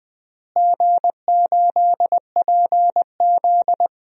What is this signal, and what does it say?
Morse code, heard as a keyed tone, about 20 wpm, 700 Hz. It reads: G8PZ
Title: DXCC-100 Call Signs - 4 Characters - Repeated 2x - 20wpm